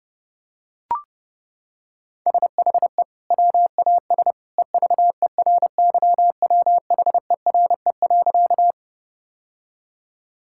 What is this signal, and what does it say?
Morse code, heard as a keyed tone, about 30 wpm, 700 Hz. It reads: SHE WAS EVERYWHERE.
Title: Sentences from Top 1600 Words 30wpm